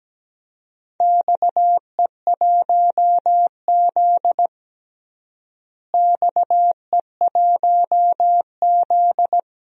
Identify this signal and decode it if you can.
Morse code, heard as a keyed tone, about 17 wpm, 700 Hz. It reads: XE1Z XE1Z
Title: DXCC-100 Call Signs - 4 Characters - Repeated 3x - 17wpm